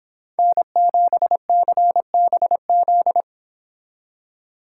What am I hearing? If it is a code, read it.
Morse code, heard as a keyed tone, about 26 wpm, 700 Hz. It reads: N7CBZ